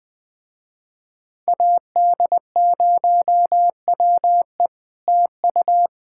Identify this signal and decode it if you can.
Morse code, heard as a keyed tone, about 20 wpm, 700 Hz. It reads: AD0WE TU
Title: Speed-Racing Common QSO Phrases Encore - 17wpm